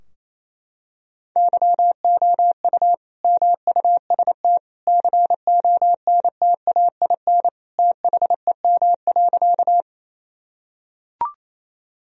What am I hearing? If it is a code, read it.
Morse code, heard as a keyed tone, about 28 wpm, 700 Hz. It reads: YOU MUST CONTAIN THEM.
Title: Sentences from Top 1000 Words - Review Part 2 - 28wpm